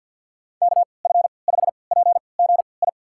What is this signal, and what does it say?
Morse code, heard as a keyed tone, about 53 wpm, spaced as for 29 wpm, 700 Hz. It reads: KF5PGI
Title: USA Call Signs - 6 Characters - 3x Character Spacing - 50wpm